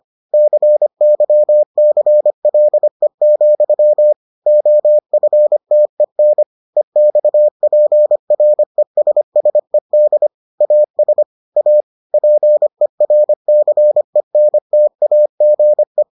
Morse code, heard as a keyed tone, about 25 wpm, 600 Hz. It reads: CYCLE, OFTEN EXPRESSED AS A PERCENTAGE